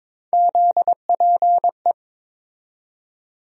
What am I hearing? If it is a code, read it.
Morse code, heard as a keyed tone, about 22 wpm, 700 Hz. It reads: ZPE
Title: USA Call Sign Postfixes - 3 Letters 22wpm